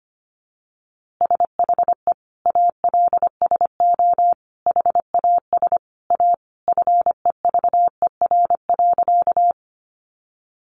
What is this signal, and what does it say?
Morse code, heard as a keyed tone, about 25 wpm, 700 Hz. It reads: SHE ALSO HAS A FEVER.
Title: Easy Phrases 25wpm